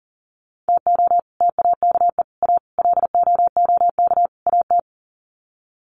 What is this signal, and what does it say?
Morse code, heard as a keyed tone, about 40 wpm, 700 Hz. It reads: TO TAKE A LOOK AT
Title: Sets of 5 Words 40wpm